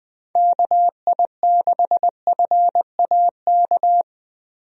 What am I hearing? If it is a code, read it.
Morse code, heard as a keyed tone, about 20 wpm, 700 Hz. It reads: KI6FAK
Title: USA Call Signs - 6 Characters - Repeated 3x - 20wpm